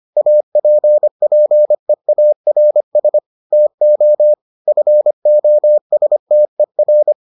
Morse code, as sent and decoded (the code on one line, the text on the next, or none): .- .--. .--. .- .-. ... / - --- / ..-. --- ... - . .-.
APPEARS TO FOSTER